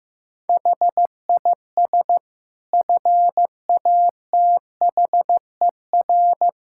Morse code, heard as a keyed tone, about 15 wpm, 700 Hz. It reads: HIS FATHER
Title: Sentences from Top 1000 Words - Review Part 5 - 15wpm